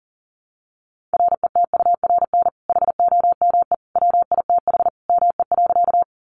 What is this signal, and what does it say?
Morse code, heard as a keyed tone, about 40 wpm, 700 Hz. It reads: RETURN HOME WITH ME.
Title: Sentences from Top 800 Words 40wpm